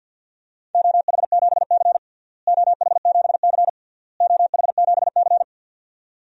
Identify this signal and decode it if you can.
Morse code, heard as a keyed tone, about 50 wpm, 700 Hz. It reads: OH7C OH7C OH7C
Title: DXCC-100 Call Signs - 4 Characters - Repeated 3x - 50wpm